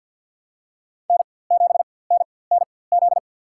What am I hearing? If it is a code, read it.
Morse code, heard as a keyed tone, about 47 wpm, spaced as for 23 wpm, 700 Hz. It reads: N7NNZ
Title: USA Call Signs - 4x Character Spacing - 50wpm